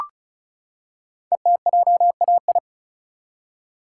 Morse code, heard as a keyed tone, about 33 wpm, 700 Hz. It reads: ETJAI
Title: Code Groups - 5 Letters - 35wpm